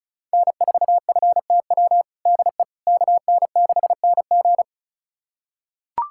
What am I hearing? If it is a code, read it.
Morse code, heard as a keyed tone, about 34 wpm, 700 Hz. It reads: N4FTW DE KN6NG
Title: USA Callsign DE Callsign - 5 Characters - 35wpm